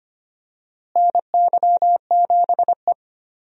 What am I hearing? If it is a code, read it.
Morse code, heard as a keyed tone, about 25 wpm, 700 Hz. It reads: NY7E
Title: USA Call Signs - 4 Characters - Repeated 2x - 25wpm